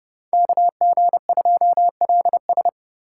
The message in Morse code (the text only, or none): KG2LS